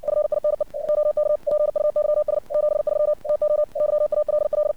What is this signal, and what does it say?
Morse code, heard as a keyed tone, about 57 wpm, 600 Hz. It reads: FINE OK PUYS 73 TO YALL
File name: archived / CRSnet